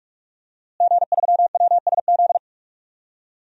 Morse code, heard as a keyed tone, about 46 wpm, 700 Hz. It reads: G3WSZ
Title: DXCC-100 Call Signs - 5 Characters - Repeated 3x - 45wpm